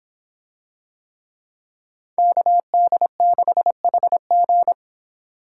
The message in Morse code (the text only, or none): KD6HG